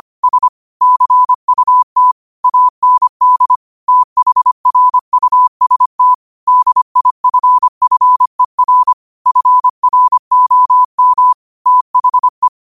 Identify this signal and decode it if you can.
Morse code, heard as a keyed tone, about 25 wpm, 1000 Hz. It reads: S CUT AND THRUST DIFFER FROM THE
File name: qotdfiles / 25wpm